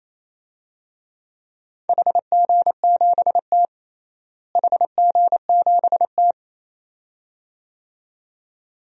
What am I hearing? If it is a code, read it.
Morse code, heard as a keyed tone, about 28 wpm, 700 Hz. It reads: HG7T HG7T
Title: DXCC-100 Call Signs - 4 Characters - Repeated 2x - 28wpm